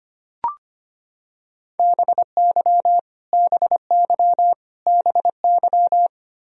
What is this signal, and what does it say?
Morse code, heard as a keyed tone, about 25 wpm, 700 Hz. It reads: BY BY BY
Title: Top 100 Words Repeated 25wpm